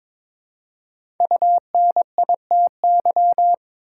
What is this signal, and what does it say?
Morse code, heard as a keyed tone, about 22 wpm, 700 Hz. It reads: UNITY
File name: 5 Letter Words - First 3 Letters Given - 22wpm